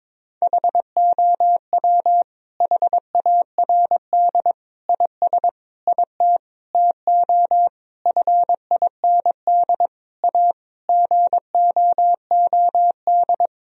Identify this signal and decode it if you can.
Morse code, heard as a keyed tone, about 22 wpm, 700 Hz. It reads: HOW HARD IS IT TO FIND A GOOD
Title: Sentences from Top 1000 Words - Review Part 1 - 22wpm